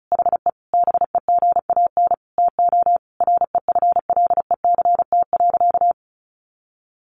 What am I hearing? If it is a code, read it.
Morse code, heard as a keyed tone, about 34 wpm, 700 Hz. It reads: HE BEGAN TO REFLECT.
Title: Sentences from Top 1500 Words 35wpm